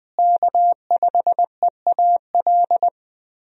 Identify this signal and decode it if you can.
Morse code, heard as a keyed tone, about 20 wpm, 700 Hz. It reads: K5EAL
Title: USA Call Signs - 5 Characters - Repeated 3x - 20wpm